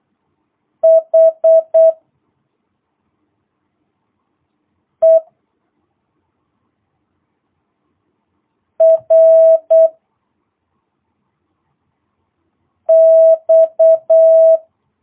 Morse code, heard as a keyed tone, about 8 wpm, 650 Hz. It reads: H E R X